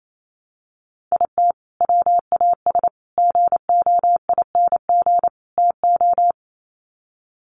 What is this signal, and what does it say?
Morse code, heard as a keyed tone, about 28 wpm, 700 Hz. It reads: IT WAS GOING TO
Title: Sets of 4 Words 28wpm